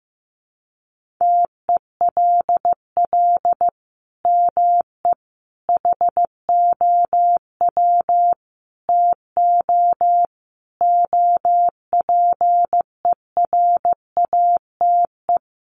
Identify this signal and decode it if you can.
Morse code, heard as a keyed tone, about 15 wpm, 700 Hz. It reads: TELL ME HOW TO OPERATE
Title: Sentences from Top 1500 Words 15wpm